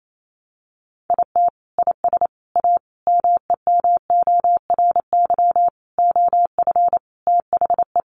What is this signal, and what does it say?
Morse code, heard as a keyed tone, about 28 wpm, 700 Hz. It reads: IT IS A MEMORY OF THE
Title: Sentences from Top 1300 Words 28wpm